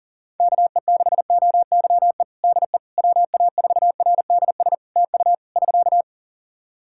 Morse code, heard as a keyed tone, about 40 wpm, 700 Hz. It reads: KE6OYE DE WA4RDS TU <SK>
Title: Practice QSO - Exchange 4 - Ending - 40wpm